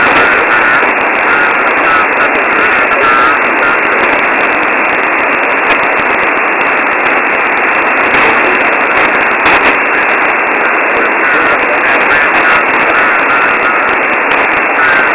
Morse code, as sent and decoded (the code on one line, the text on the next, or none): none